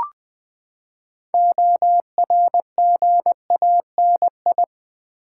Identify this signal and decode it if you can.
Morse code, heard as a keyed tone, about 20 wpm, 700 Hz. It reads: ORGANI
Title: Build A Word - 6 Letters At A Time - 12-Letter Words - 20wpm